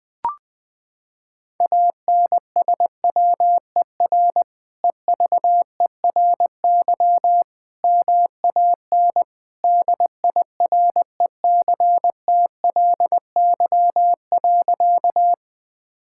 Morse code, heard as a keyed tone, about 21 wpm, 700 Hz. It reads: ANSWER EVERY MAN DIRECTLY.
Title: Sentences from Top 1000 Words - Review Part 5 - 20wpm